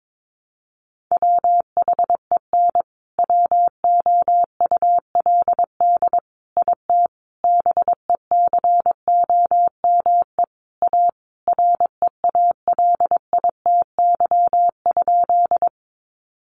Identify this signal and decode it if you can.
Morse code, heard as a keyed tone, about 22 wpm, 700 Hz. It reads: WHEN WOULD IT BECOME A REALITY?
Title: Sentences from Top 1200 Words 22wpm